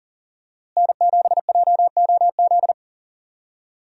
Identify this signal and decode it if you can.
Morse code, heard as a keyed tone, about 40 wpm, 700 Hz. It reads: N7JOZ
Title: USA Call Signs - 5 Characters - Repeated 2x - 40wpm